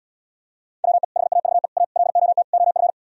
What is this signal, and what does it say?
Morse code, heard as a keyed tone, about 77 wpm, 700 Hz. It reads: WE HAVE A LOT OF